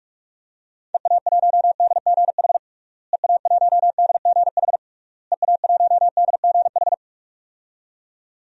Morse code, heard as a keyed tone, about 46 wpm, 700 Hz. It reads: EA1DGH EA1DGH EA1DGH
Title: DXCC-100 Call Signs - 6 Characters - Repeated 3x - 45wpm